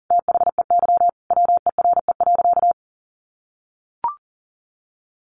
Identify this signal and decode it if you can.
Morse code, heard as a keyed tone, about 37 wpm, 700 Hz. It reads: THEY WERE.
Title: Sentences from Top 1000 Words 40wpm